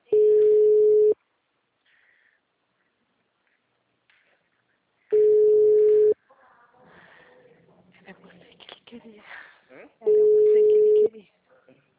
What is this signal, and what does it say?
Ringback tone (Europe, 425 Hz), with speech.
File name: out- -2010- - - .12